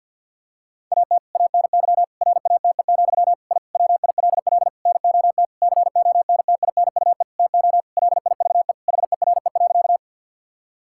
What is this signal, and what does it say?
Morse code, heard as a keyed tone, about 50 wpm, 700 Hz. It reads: AT ANY RATE, I WILL NOT CONTINUE TO LIVE HERE.
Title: Sentences from Top 1000 Words - Review Part 6 - 50wpm